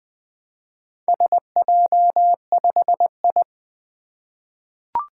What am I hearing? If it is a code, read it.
Morse code, heard as a keyed tone, about 21 wpm, 700 Hz. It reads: SJ5I
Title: Four Letter-Number Combinations - 20wpm